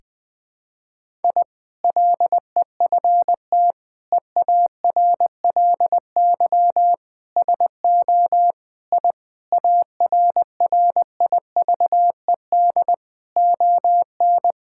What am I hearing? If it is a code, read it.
Morse code, heard as a keyed tone, about 20 wpm, 700 Hz. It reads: I LEFT EARLY SO I ARRIVED ON
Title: Sentences with 2-Word Phrases - 20wpm